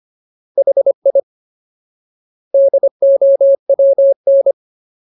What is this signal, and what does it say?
Morse code, heard as a keyed tone, about 25 wpm, 550 Hz. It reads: HI DOWN
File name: Ultimate Word List / 25WPM Ultimate